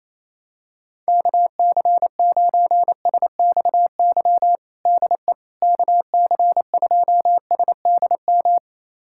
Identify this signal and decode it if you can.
Morse code, heard as a keyed tone, about 28 wpm, 700 Hz. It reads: KC9SXY DE KC2SDM
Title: USA Callsign DE Callsign - 6 Characters - 28wpm